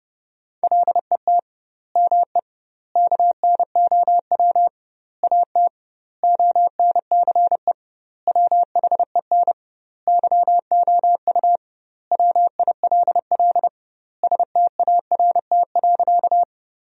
Morse code, heard as a keyed tone, about 31 wpm, 700 Hz. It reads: LET ME KNOW AT ONCE WHEN YOU WILL START.
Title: 2x Word-Spacing - Sentences from Top 300 Words - 30wpm